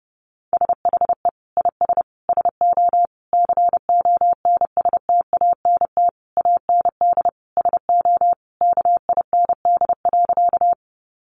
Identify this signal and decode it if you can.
Morse code, heard as a keyed tone, about 28 wpm, 700 Hz. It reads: SHE IS SO CONSTANT AND SO KIND.
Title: Sentences from Top 1500 Words 30wpm